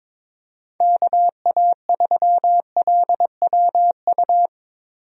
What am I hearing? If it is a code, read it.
Morse code, heard as a keyed tone, about 22 wpm, 700 Hz. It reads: KA3LWU
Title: USA Call Signs - 6 Characters - Repeated 2x - 22wpm